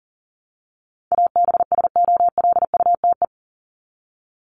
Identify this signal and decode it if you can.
Morse code, heard as a keyed tone, about 40 wpm, 700 Hz. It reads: ABSOLUTE
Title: LY Suffix - Top 100 Words with Base Words 40wpm